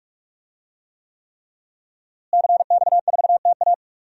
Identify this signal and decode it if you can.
Morse code, heard as a keyed tone, about 44 wpm, 700 Hz. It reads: CX4TA
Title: DXCC-100 Call Signs - 5 Characters - Repeated 3x - 45wpm